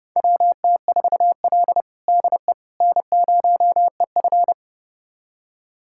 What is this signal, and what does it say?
Morse code, heard as a keyed tone, about 29 wpm, 700 Hz. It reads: WT4L DE N0EF
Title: USA Callsign DE Callsign - 4 Characters - 30wpm